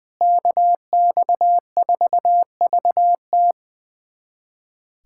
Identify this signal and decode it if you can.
Morse code, heard as a keyed tone, about 20 wpm, 700 Hz. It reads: KX4VT
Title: USA Callsign DE Callsign 20wpm